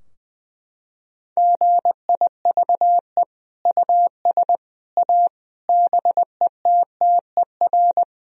Morse code, heard as a keyed tone, about 20 wpm, 700 Hz. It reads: GIVE US A BETTER
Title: Sentences from Top 1000 Words - Review Part 4 - 20wpm